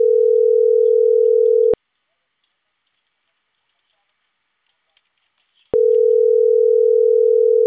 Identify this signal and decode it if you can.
Ringback tone (North America), 440 + 480 Hz.